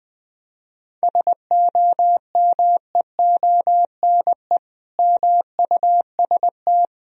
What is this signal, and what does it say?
Morse code, heard as a keyed tone, about 20 wpm, 700 Hz. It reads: SOMEONE MUST
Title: Sentences from Top 1000 Words - Review Part 4 - 20wpm